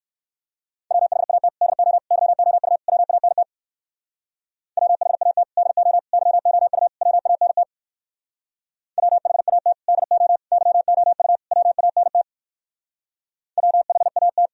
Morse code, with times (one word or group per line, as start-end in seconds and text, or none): 0.9-1.5 WHAT
1.6-2.0 DO
2.1-2.8 YOU
2.9-3.4 WANT
4.8-5.4 WHAT
5.6-6.0 DO
6.1-6.9 YOU
7.0-7.6 WANT
9.0-9.7 WHAT
9.9-10.4 DO
10.5-11.4 YOU
11.5-12.2 WANT
13.6-14.5 WHAT